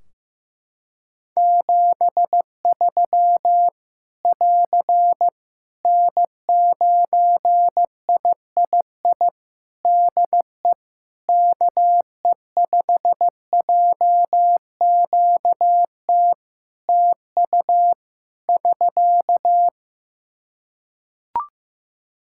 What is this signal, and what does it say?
Morse code, heard as a keyed tone, about 15 wpm, 700 Hz. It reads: 73 <AR> N9III DE KE5JQT TU <SK>
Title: Practice QSO - Exchange 4 - Ending - 15wpm